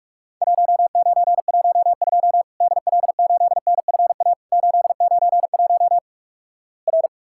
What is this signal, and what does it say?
Morse code, heard as a keyed tone, about 46 wpm, 700 Hz. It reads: 1912 DL8NFA 891 R
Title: Worked All Europe DX Contest - QTC Exchange - 45wpm